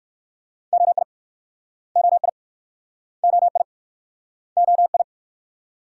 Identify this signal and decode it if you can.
Morse code, heard as a keyed tone, about 54 wpm, 690 Hz. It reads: OI OI OI OI
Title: Speed-Racing Multi-letter Phonemes - 45wpm